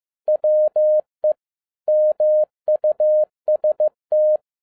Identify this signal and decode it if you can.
Morse code, heard as a keyed tone, about 15 wpm, 610 Hz. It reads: WE MUST